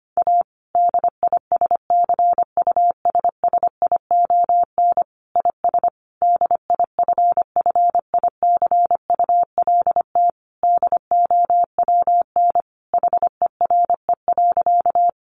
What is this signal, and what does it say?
Morse code, heard as a keyed tone, about 25 wpm, 700 Hz. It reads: A DISCUSSION IS DIFFICULT DOWN HERE.
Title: Sentences from Top 1100 Words 25wpm